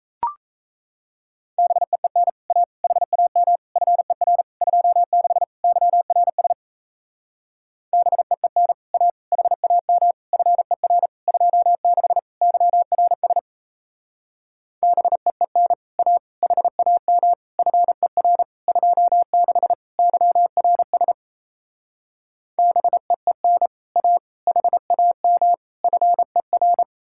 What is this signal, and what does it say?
Morse code, heard as a keyed tone, about 37 wpm, 700 Hz. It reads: BEEN A HAM FER 26 YRS BEEN A HAM FER 26 YRS BEEN A HAM FER 26 YRS BEEN A HAM FER